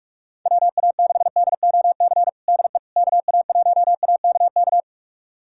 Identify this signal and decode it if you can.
Morse code, heard as a keyed tone, about 46 wpm, 700 Hz. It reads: WA6DOC DE KA1AKK
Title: USA Callsign DE Callsign - 6 Characters - 45wpm